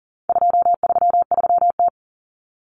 Morse code, heard as a keyed tone, about 40 wpm, 700 Hz. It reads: 233T
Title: Common Cut Numbers - 4-Digit 40wpm